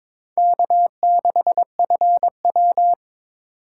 Morse code, heard as a keyed tone, about 26 wpm, 700 Hz. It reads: K6FW
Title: Speed Racing - USA 1x2 Call Signs 22wpm